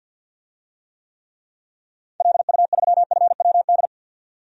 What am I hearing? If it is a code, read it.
Morse code, heard as a keyed tone, about 50 wpm, 700 Hz. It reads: LU3FWD
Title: DXCC-100 Call Signs - 6 Characters - 50wpm